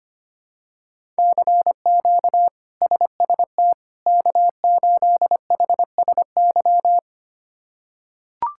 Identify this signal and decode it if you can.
Morse code, heard as a keyed tone, about 25 wpm, 700 Hz. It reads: CQ SST K8HSY
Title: K1USN SST QSO Practice - 25wpm